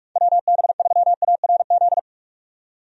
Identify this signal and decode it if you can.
Morse code, heard as a keyed tone, about 46 wpm, 700 Hz. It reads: WB3ARZ
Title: USA Call Signs - 6 Characters - 45wpm